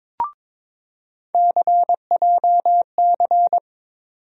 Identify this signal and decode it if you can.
Morse code, heard as a keyed tone, about 23 wpm, 700 Hz. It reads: CJC
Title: USA Call Sign Postfixes - 3 Letters 22wpm